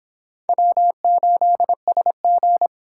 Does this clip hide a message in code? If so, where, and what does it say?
Yes: Morse code from 0.5 to 2.7 s, heard as a keyed tone, about 26 wpm, 700 Hz. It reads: W8SG